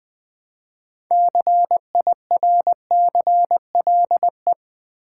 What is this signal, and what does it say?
Morse code, heard as a keyed tone, about 20 wpm, 700 Hz. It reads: CIRCLE
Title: Build A Word - 2 Letters At A Time - 6-Letter Words - 20wpm